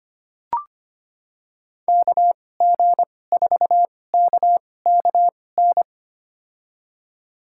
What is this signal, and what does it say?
Morse code, heard as a keyed tone, about 25 wpm, spaced as for 18 wpm, 700 Hz. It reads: KG4KKN
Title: USA Call Signs - 6 Characters - 2x Character Spacing - 25wpm